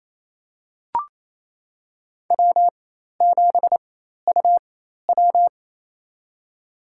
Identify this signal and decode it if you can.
Morse code, heard as a keyed tone, about 27 wpm, spaced as for 13 wpm, 700 Hz. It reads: W7UW
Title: USA Call Signs - 4 Characters - 4x Character Spacing - 28wpm